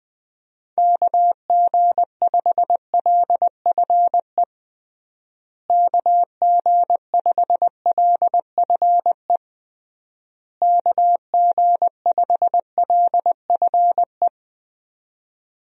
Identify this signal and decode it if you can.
Morse code, heard as a keyed tone, about 21 wpm, 700 Hz. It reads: KG5LFE KG5LFE KG5LFE
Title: USA Call Signs - 6 Characters - Repeated 3x - 20wpm